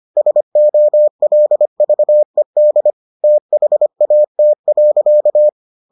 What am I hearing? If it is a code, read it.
Morse code, heard as a keyed tone, about 25 wpm, 600 Hz. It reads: SOLVED THAT.